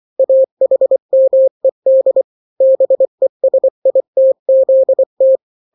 Morse code, heard as a keyed tone, about 24 wpm, 530 Hz. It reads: AHMED BESITZT
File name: daily / 23wpm